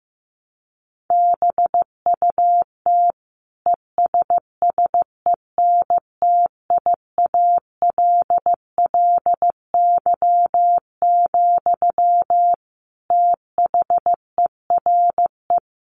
Morse code, heard as a keyed tone, about 15 wpm, 700 Hz. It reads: BUT ESSENTIALLY, THERE
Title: Sentences from Top 1300 Words 15wpm